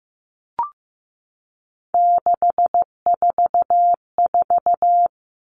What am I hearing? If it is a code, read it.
Morse code, heard as a keyed tone, about 15 wpm, 700 Hz. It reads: 644
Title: Numbers - 3-Digit 15wpm